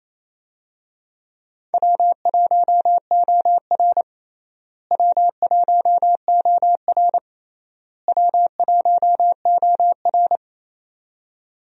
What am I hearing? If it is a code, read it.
Morse code, heard as a keyed tone, about 28 wpm, 700 Hz. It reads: W1OR W1OR W1OR
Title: USA Call Signs - 4 Characters - Repeated 3x - 28wpm